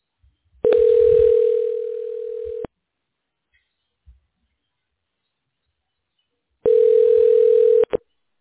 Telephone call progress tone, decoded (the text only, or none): ringback tone